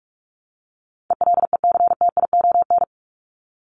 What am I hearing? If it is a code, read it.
Morse code, heard as a keyed tone, about 44 wpm, 700 Hz. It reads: ELECTION
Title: Top 1251-1500 Words 45wpm